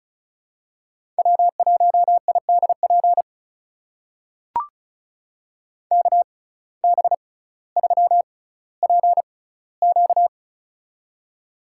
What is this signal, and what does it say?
Morse code, heard as a keyed tone, about 34 wpm, spaced as for 18 wpm, 700 Hz. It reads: W1IDP KB3PQ
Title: USA Call Signs - 6x Character Spacing - 35wpm